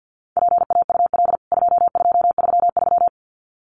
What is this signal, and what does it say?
Morse code, heard as a keyed tone, about 50 wpm, 700 Hz. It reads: PAUL 2233
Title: CWT Exchange 50wpm